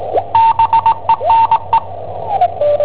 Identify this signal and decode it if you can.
Morse code, heard as a keyed tone, about 20 wpm, 910 Hz. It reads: BENE
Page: Cassetta esercizi preparata da I7OHP per IW5ELP